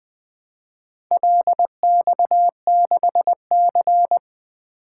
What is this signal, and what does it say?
Morse code, heard as a keyed tone, about 20 wpm, 700 Hz. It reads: LX6C
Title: Four Letter-Number Combinations - 20wpm